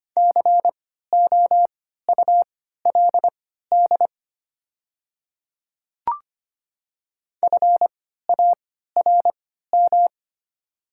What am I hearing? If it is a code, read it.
Morse code, heard as a keyed tone, about 25 wpm, spaced as for 14 wpm, 700 Hz. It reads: COULD FARM
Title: 3x Character Spacing - Top 500 Words - 25wpm